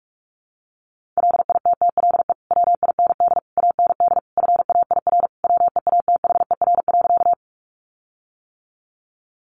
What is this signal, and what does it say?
Morse code, heard as a keyed tone, about 44 wpm, 700 Hz. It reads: LITTLE WIND AND FAIR WEATHER.
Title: Sentences from Top 1300 Words 45wpm